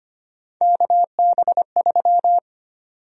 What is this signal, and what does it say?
Morse code, heard as a keyed tone, about 25 wpm, 700 Hz. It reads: KB3
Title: USA Call Sign Prefixes - 25wpm